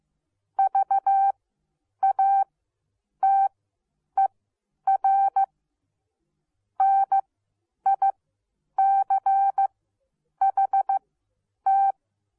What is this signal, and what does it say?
Morse code, heard as a keyed tone, about 15 wpm, spaced as for 9 wpm, 780 Hz. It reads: VATER NICHT